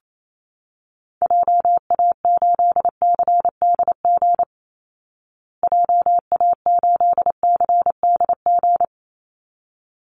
Morse code, heard as a keyed tone, about 28 wpm, 700 Hz. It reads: JA8CDG JA8CDG
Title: Call Signs Repeated 3x - 28wpm